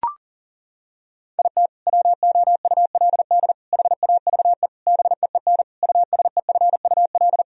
Morse code, heard as a keyed tone, about 40 wpm, 700 Hz. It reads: IT WOULD HAVE BEEN USEFUL
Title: Sentences from Top 1000 Words - Review Part 4 - 40wpm